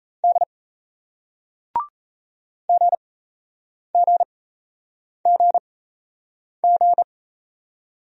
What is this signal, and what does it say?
Morse code, heard as a keyed tone, about 38 wpm, spaced as for 8 wpm, 700 Hz. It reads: D GGGG